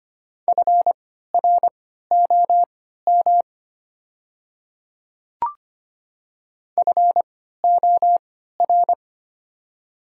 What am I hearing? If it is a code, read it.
Morse code, heard as a keyed tone, about 25 wpm, spaced as for 14 wpm, 690 Hz. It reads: FROM FOR
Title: 3x Character Spacing - Top 100 Words - 25wpm